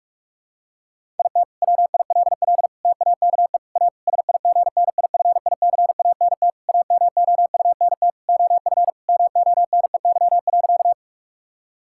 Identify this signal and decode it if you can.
Morse code, heard as a keyed tone, about 46 wpm, 700 Hz. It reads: IT WILL TAKE A SIGNIFICANT AMOUNT OF MONEY.
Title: Sentences from Top 1000 Words - Review Part 6 - 45wpm